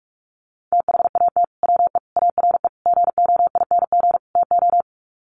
Morse code, heard as a keyed tone, about 44 wpm, 700 Hz. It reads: THAT WE ARE GOING TO
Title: Sets of 5 Words Encore 45wpm